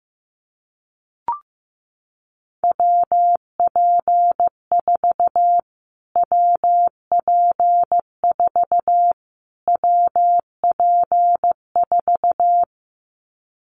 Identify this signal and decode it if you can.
Morse code, heard as a keyed tone, about 15 wpm, 700 Hz. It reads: WP4 WP4 WP4
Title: Callsign Prefixes for Top 25 DXCC - Repeated 3x - 15wpm